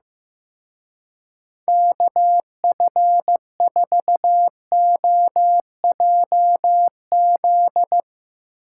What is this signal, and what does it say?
Morse code, heard as a keyed tone, about 15 wpm, 700 Hz. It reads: KF4OJZ
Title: Speed Racing - USA Call Signs 15wpm